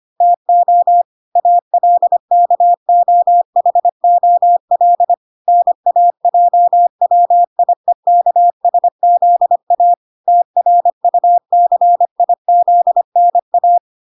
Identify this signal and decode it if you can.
Morse code, heard as a keyed tone, about 25 wpm, 700 Hz. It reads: TO ALKOHOL NAJWIEKSZA TRUCIZNA